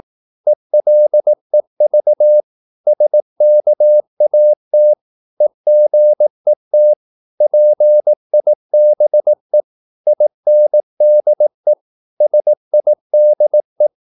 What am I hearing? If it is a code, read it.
Morse code, heard as a keyed tone, about 18 wpm, 600 Hz. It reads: ELEV SKAT EGET PIBE INDE SIDE
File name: Korte ord DK 18wpm